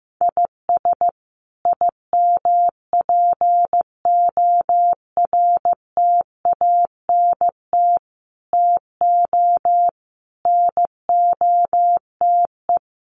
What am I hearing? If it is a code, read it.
Morse code, heard as a keyed tone, about 15 wpm, 700 Hz. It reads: IS IMPORTANT TO NOTE
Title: Sets of 5 Words 15wpm